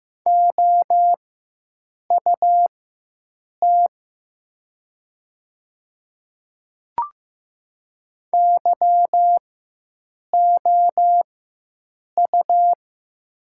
Morse code, heard as a keyed tone, about 15 wpm, spaced as for 7 wpm, 700 Hz. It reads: OUT YOU
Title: 4x Character Spacing - Top 100 Words - 15wpm